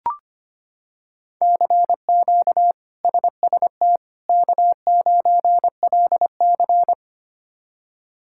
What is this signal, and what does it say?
Morse code, heard as a keyed tone, about 25 wpm, 700 Hz. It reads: CQ SST K9LC
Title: K1USN SST QSO Practice - 25wpm